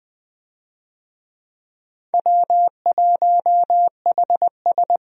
Morse code, heard as a keyed tone, about 20 wpm, 700 Hz. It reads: W1HS
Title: USA Call Signs - 4 Characters - 20wpm